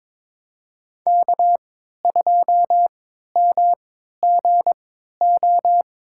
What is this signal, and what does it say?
Morse code, heard as a keyed tone, about 22 wpm, spaced as for 12 wpm, 700 Hz. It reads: K2MGO
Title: USA Call Signs - 3x Character Spacing - 22wpm